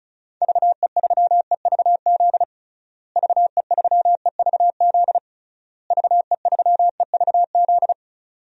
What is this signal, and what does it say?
Morse code, heard as a keyed tone, about 34 wpm, 700 Hz. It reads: VE3EVZ VE3EVZ VE3EVZ
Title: DXCC-100 Call Signs - 6 Characters - Repeated 3x - 35wpm